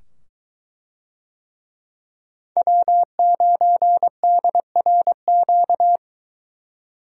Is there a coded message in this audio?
Yes: Morse code from 2.6 to 6.0 s, heard as a keyed tone, about 22 wpm, 700 Hz. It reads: W9DRQ